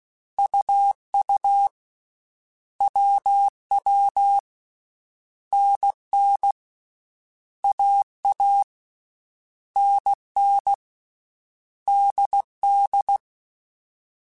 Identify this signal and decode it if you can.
Morse code, heard as a keyed tone, about 16 wpm, 800 Hz. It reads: UU WW NN AA NN DD
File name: Basic Sets / Basic Letter Groups Collective7 / AWUNDG@13